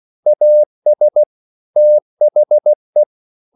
Morse code, heard as a keyed tone, about 16 wpm, 600 Hz. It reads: AS THE
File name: CW / Practice / EnglishSayings16